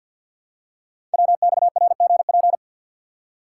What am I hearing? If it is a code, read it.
Morse code, heard as a keyed tone, about 48 wpm, 700 Hz. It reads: WXRGP
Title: Code Groups - 5 Letters - 50wpm